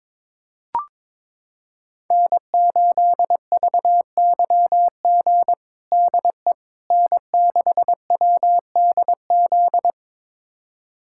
Morse code, heard as a keyed tone, about 22 wpm, 700 Hz. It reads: N8VYG DE N6WDZ
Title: USA Callsign DE Callsign - 5 Characters - 22wpm